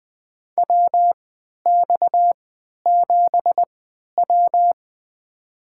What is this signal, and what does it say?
Morse code, heard as a keyed tone, about 20 wpm, spaced as for 11 wpm, 700 Hz. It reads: WX7W
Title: USA Call Signs - 4 Characters - 3x Character Spacing - 20wpm